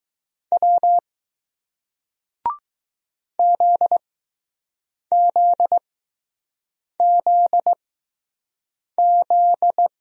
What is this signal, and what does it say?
Morse code, heard as a keyed tone, about 20 wpm, spaced as for 6 wpm, 700 Hz. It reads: W ZZZZ